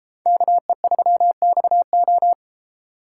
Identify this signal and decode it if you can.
Morse code, heard as a keyed tone, about 34 wpm, 700 Hz. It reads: KE3XO
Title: Speed Racing - USA 2x2 Call Signs 28wpm